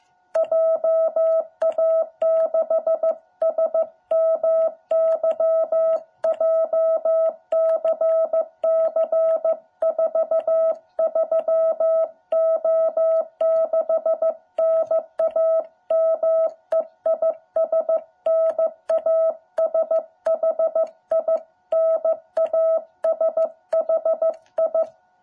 Morse code, heard as a keyed tone, about 15 wpm, 660 Hz. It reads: JA6SMY JCC43O6 NAME IS NASHI NASHI
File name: jh8cbh / 2021cw